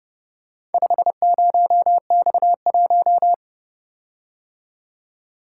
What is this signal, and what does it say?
Morse code, heard as a keyed tone, about 29 wpm, 700 Hz. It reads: 50X1